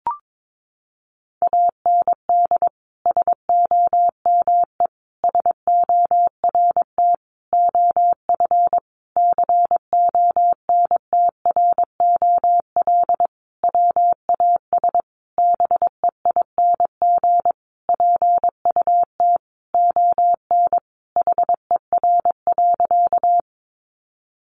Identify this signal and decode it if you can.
Morse code, heard as a keyed tone, about 22 wpm, 700 Hz. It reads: AND SOME SORT OF CONTROL WAS BEING PUT ON HER.
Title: Sentences from Top 1300 Words 22wpm